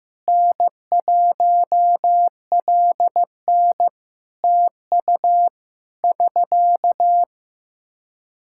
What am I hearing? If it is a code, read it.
Morse code, heard as a keyed tone, about 15 wpm, 700 Hz. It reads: N1LN TU <SK>
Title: Practice QSO - Exchange 4 - Ending - 15wpm